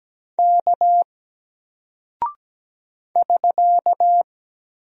Morse code, heard as a keyed tone, about 17 wpm, 700 Hz. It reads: K <SK>
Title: Single Character (Letter:Number:Punctuation:Prosign) 17wpm